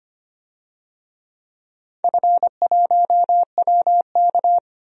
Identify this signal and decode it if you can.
Morse code, heard as a keyed tone, about 25 wpm, 700 Hz. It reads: F1WK
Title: DXCC-100 Call Signs - 4 Characters - 25wpm